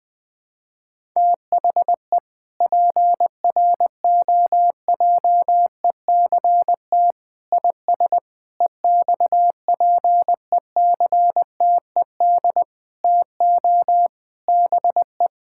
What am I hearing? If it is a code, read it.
Morse code, heard as a keyed tone, about 20 wpm, 700 Hz. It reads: THE PROJECT IS EXPECTED TO BE
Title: Sentences with 8-Word Phrases - 20wpm